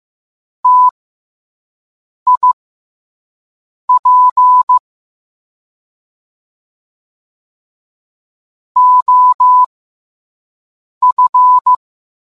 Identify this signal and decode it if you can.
Morse code, heard as a keyed tone, about 15 wpm, spaced as for 5 wpm, 1000 Hz. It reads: TIP OF